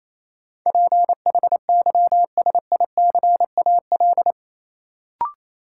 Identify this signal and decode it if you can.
Morse code, heard as a keyed tone, about 28 wpm, 700 Hz. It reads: PHYSICAL
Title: Sentences from Top 1400 Words - New Word Drill Alt - 28wpm